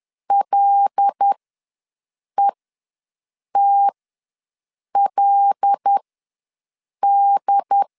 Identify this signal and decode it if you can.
Morse code, heard as a keyed tone, about 10 wpm, spaced as for 6 wpm, 800 Hz. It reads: LETLD
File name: LEZIONI / 5 LBVC / 5 L ws 10wpm 3636